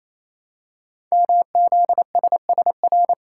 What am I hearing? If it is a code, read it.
Morse code, heard as a keyed tone, about 27 wpm, 700 Hz. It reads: MZSSR
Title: Code Groups - 5 Letters - 28wpm